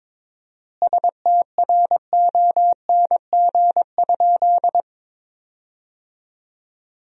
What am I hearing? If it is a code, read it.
Morse code, heard as a keyed tone, about 22 wpm, 700 Hz. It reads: STRONG?
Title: 4x Word-Spacing - Sentences from Top 500 Words - 22wpm